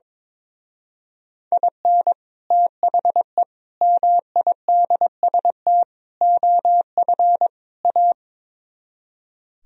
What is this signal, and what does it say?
Morse code, heard as a keyed tone, about 22 wpm, 700 Hz. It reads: IN THE MIDST OF A
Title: Speed-Racing - Sets of 5 Words 22wpm